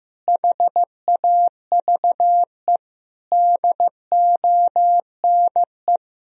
Morse code, heard as a keyed tone, about 15 wpm, 700 Hz. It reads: HAVE DONE
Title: Sentences from Top 1000 Words - Review Part 2 - 15wpm